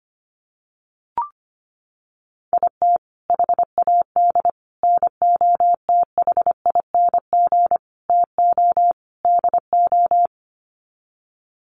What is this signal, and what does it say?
Morse code, heard as a keyed tone, about 25 wpm, 700 Hz. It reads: IT HAD NOTHING TO DO
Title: Sets of 5 Words Encore2 25wpm